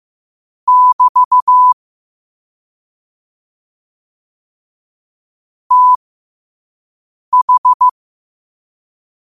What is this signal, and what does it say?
Morse code, heard as a keyed tone, about 15 wpm, spaced as for 5 wpm, 1000 Hz. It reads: <BT> TH
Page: New quotes every day in morse code at 5 Words per minute.